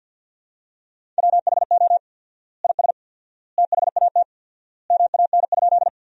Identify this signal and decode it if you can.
Morse code, heard as a keyed tone, about 50 wpm, 700 Hz. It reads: WHO IS THAT MAN?
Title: 4x Word-Spacing - Sentences from Top 200 Words - 50wpm